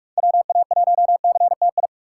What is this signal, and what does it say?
Morse code, heard as a keyed tone, about 42 wpm, 700 Hz. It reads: WA1CTI